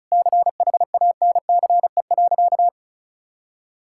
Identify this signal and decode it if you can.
Morse code, heard as a keyed tone, about 36 wpm, 700 Hz. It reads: CHANCE.
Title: Sentences from Top 1000 Words - Review Part 5 - 35wpm